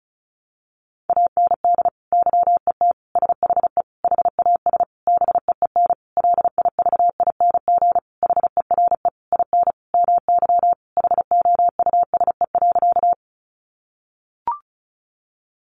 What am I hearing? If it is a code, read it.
Morse code, heard as a keyed tone, about 34 wpm, 700 Hz. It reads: AND YET SHE HAS BEEN LIVING HERE IN MY HOUSE.
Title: Sentences from Top 1100 Words 35wpm